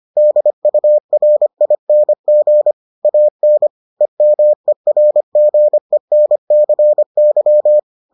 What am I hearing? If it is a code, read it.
Morse code, heard as a keyed tone, about 25 wpm, 600 Hz. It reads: DURING AN EMERGENCY